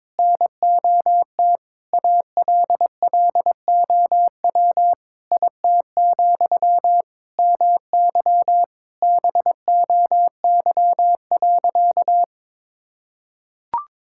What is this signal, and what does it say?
Morse code, heard as a keyed tone, about 22 wpm, 700 Hz. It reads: NOT ALLOW IT, MY BOY.
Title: Sentences from Top 1000 Words - Review Part 5 - 22wpm